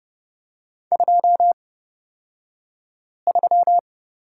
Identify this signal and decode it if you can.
Morse code, heard as a keyed tone, about 29 wpm, spaced as for 5 wpm, 690 Hz. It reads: 23
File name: Lesson 36d - BT - Two Characters 30wpm